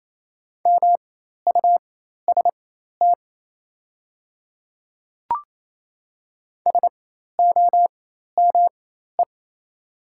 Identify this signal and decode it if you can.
Morse code, heard as a keyed tone, about 27 wpm, spaced as for 13 wpm, 690 Hz. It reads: MUST SOME
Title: 4x Character Spacing - Top 200 Words - 28wpm